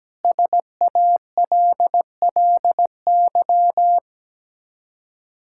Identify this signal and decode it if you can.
Morse code, heard as a keyed tone, about 17 wpm, 700 Hz. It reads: SALLY
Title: K1USN SST QSO Practice 17wpm